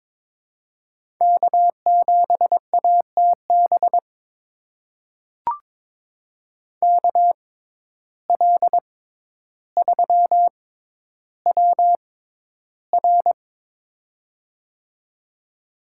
Morse code, heard as a keyed tone, about 22 wpm, spaced as for 11 wpm, 700 Hz. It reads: K7ATB KL3WR
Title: USA Call Signs - 5 Characters - 6x Character Spacing - 22wpm